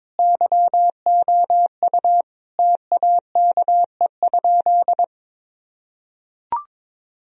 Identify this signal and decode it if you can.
Morse code, heard as a keyed tone, about 22 wpm, 700 Hz. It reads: YOU TAKE?
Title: Sentences from Top 1000 Words - Review Part 2 - 22wpm